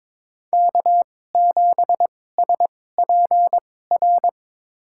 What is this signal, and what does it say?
Morse code, heard as a keyed tone, about 22 wpm, spaced as for 16 wpm, 700 Hz. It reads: K7SPR
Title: USA Call Signs - 5 Characters - 2x Character Spacing - 22wpm